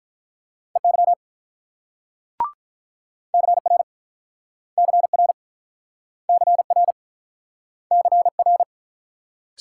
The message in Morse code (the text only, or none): EY CR CR CR CR